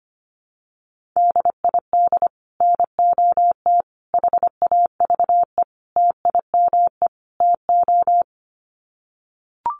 Morse code, heard as a keyed tone, about 25 wpm, 700 Hz. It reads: DID NOT HAVE TIME TO
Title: Sets of 5 Words Encore 25wpm